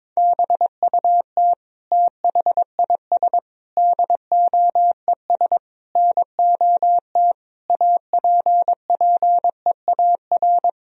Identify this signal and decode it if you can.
Morse code, heard as a keyed tone, about 22 wpm, 700 Hz. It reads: BUT THIS DOES NOT APPEAR
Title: Sentences from Top 1000 Words - Review Part 2 - 22wpm